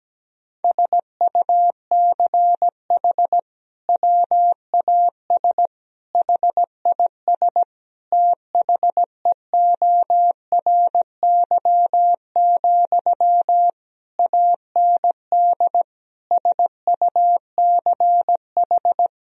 Morse code, heard as a keyed tone, about 17 wpm, 700 Hz. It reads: SUCH WAS HIS THEORY, AND SUCH
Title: Sentences from Top 1000 Words - Review Part 4 - 17wpm